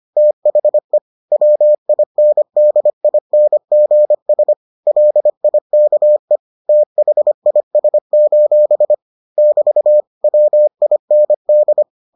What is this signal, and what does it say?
Morse code, heard as a keyed tone, about 25 wpm, 600 Hz. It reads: THE WINDINGS LIKE THIS: <BT> WIND